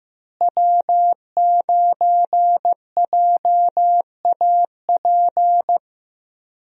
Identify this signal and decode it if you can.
Morse code, heard as a keyed tone, about 15 wpm, 700 Hz. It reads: W9JAP
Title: USA Call Signs - 5 Characters - Repeated 3x - 15wpm